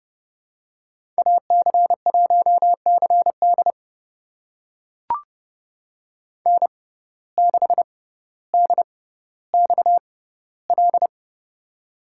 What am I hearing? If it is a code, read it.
Morse code, heard as a keyed tone, about 30 wpm, spaced as for 15 wpm, 700 Hz. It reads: AC1CD N6DXL
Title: USA Call Signs - 6x Character Spacing - 30wpm